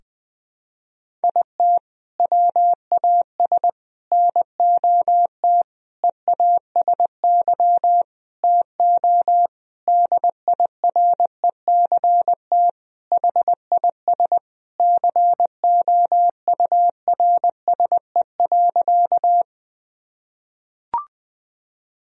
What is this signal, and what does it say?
Morse code, heard as a keyed tone, about 20 wpm, 700 Hz. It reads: IT WAS NOT EASY TO DIRECT HIS COURSE.
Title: Sentences from Top 1000 Words - Review Part 2 - 20wpm